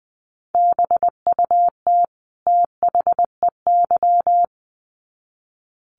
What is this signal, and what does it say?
Morse code, heard as a keyed tone, about 20 wpm, 700 Hz. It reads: BUT THEY
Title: Sets of 2 Words 20wpm